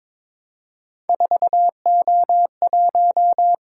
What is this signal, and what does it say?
Morse code, heard as a keyed tone, about 26 wpm, 700 Hz. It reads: 4O1
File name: Speed-Racing - Three Letter-Number Combinations - 22wpm